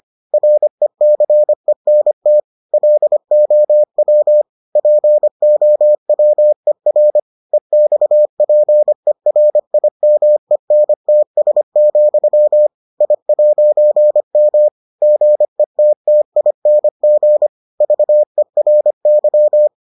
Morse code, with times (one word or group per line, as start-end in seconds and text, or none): 0.3-2.4 RECENT
2.7-4.4 LOW
4.8-7.2 POWER
7.5-12.7 EXPERIMENTS,
13.0-14.7 I'M
15.0-17.5 GETTING
17.8-19.7 VERY